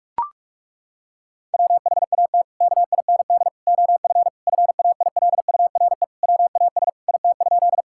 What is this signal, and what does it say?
Morse code, heard as a keyed tone, about 42 wpm, 700 Hz. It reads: WHAT KIND OF FAILURE WAS IT?
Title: Sentences from Top 1000 Words - Review Part 6 - 45wpm